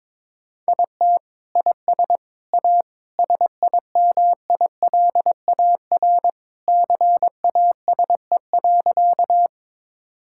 Morse code, heard as a keyed tone, about 21 wpm, 700 Hz. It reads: IT IS A SIMILAR CASE.
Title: Sentences from Top 1000 Words - Review Part 5 - 22wpm